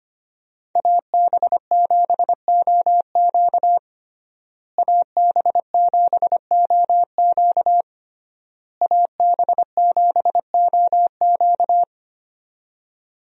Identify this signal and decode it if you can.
Morse code, heard as a keyed tone, about 25 wpm, 700 Hz. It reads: AB7OQ AB7OQ AB7OQ
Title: USA Call Signs - 5 Characters - Repeated 3x - 25wpm